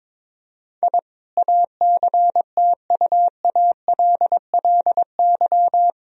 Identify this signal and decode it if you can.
Morse code, heard as a keyed tone, about 22 wpm, 700 Hz. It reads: I ACTUALLY
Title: Sentences from Top 1000 Words - Review Part 4 - 22wpm